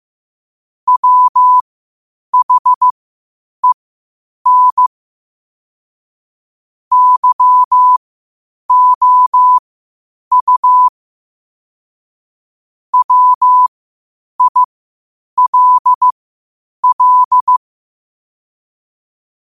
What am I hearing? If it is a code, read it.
Morse code, heard as a keyed tone, about 15 wpm, spaced as for 8 wpm, 1000 Hz. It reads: WHEN YOU WILL
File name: qotdfiles / 8wpm